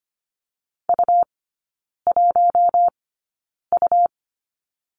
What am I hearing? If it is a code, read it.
Morse code, heard as a keyed tone, about 25 wpm, spaced as for 9 wpm, 700 Hz. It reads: U1U
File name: Lesson 17e - Letter u - Three Characters 25wpm